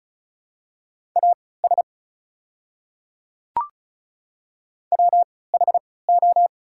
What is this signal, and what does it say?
Morse code, heard as a keyed tone, about 34 wpm, spaced as for 20 wpm, 690 Hz. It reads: AS WHO
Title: 3x Character Spacing - Top 200 Words - 35wpm